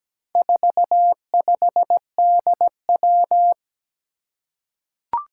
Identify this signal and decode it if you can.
Morse code, heard as a keyed tone, about 17 wpm, 700 Hz. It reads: 45DW
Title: Four Challenging Letter-Number Combinations 17wpm